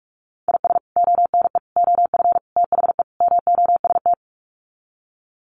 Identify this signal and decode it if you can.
Morse code, heard as a keyed tone, about 44 wpm, 700 Hz. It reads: IS ONE OF THE MOST
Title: Sets of 5 Words 45wpm